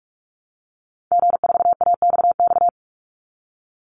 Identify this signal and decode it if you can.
Morse code, heard as a keyed tone, about 44 wpm, 700 Hz. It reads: G4AXX
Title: Call Signs - 45wpm